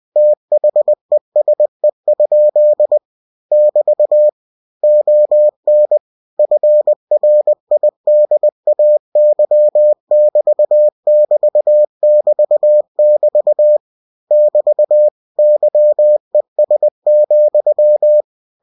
Morse code, heard as a keyed tone, about 20 wpm, 600 Hz. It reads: THESE? <BT> ON FRIDAY<BT><BT><BT><BT> <BT> YES,
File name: cw-books / winnie-the-pooh / winnie-20wpm-page-0068